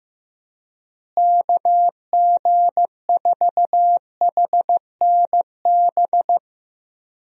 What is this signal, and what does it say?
Morse code, heard as a keyed tone, about 15 wpm, 700 Hz. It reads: KG4HNB
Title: USA Call Signs - 6 Characters - Repeated 3x - 15wpm